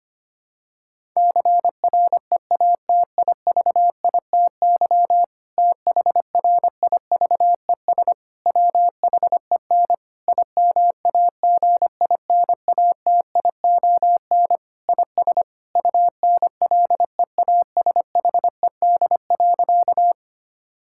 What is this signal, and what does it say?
Morse code, heard as a keyed tone, about 25 wpm, 700 Hz. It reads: CREATIVITY THRIVES WHEN IMAGINATION IS UNLEASHED.
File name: practice / 25wpm / Difficult Sentences with Hints - 25wpm